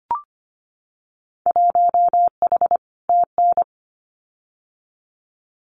Morse code, heard as a keyed tone, about 25 wpm, 700 Hz. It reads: 1H TN
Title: Winter Field Day Exchanges 25wpm